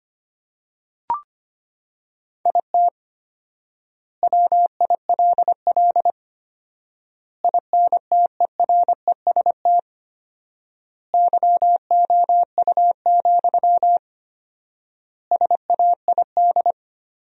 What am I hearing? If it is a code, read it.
Morse code, heard as a keyed tone, about 25 wpm, 700 Hz. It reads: IT WILL INTEREST YOU, SAID
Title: 4x Word-Spacing - Sentences from Top 400 Words - 25wpm